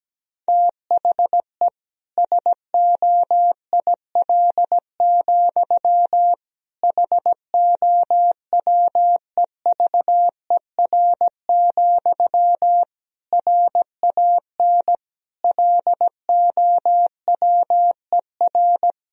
Morse code, heard as a keyed tone, about 17 wpm, 700 Hz. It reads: THE SOIL, HOWEVER, RAN LOWER
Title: Sentences from Top 1000 Words - Review Part 4 - 17wpm